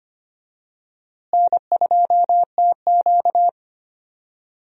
Morse code, heard as a keyed tone, about 25 wpm, 700 Hz. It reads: N2TQ
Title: USA Call Signs - 4 Characters - Repeated 2x - 25wpm